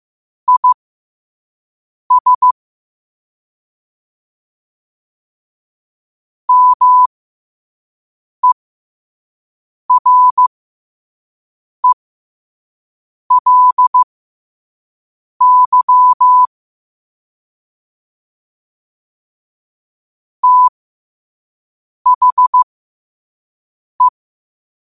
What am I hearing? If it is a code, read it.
Morse code, heard as a keyed tone, about 15 wpm, spaced as for 5 wpm, 1000 Hz. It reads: IS MERELY THE
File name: 5wpm